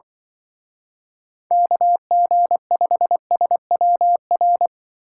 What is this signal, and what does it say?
Morse code, heard as a keyed tone, about 24 wpm, 700 Hz. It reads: KG5SWR
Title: Speed Racing - USA 2x3 Call Signs 20wpm